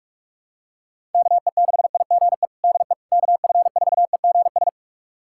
Morse code, heard as a keyed tone, about 47 wpm, 690 Hz. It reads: KE6IGE DE KF4EGS
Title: USA Callsign DE Callsign - 6 Characters - 45wpm